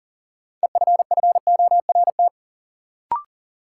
Morse code, heard as a keyed tone, about 37 wpm, 700 Hz. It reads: EFFORT
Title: Sentences from Top 1200 Words - New Word Drill - 40wpm